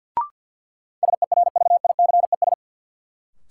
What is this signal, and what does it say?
Morse code, heard as a keyed tone, about 53 wpm, 700 Hz. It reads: SERVICES
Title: Sets of 5 Words - New Word Drill - 50wpm